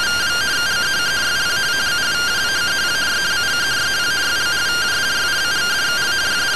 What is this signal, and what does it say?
Two-tone teleprinter signal, about 170 Hz shift.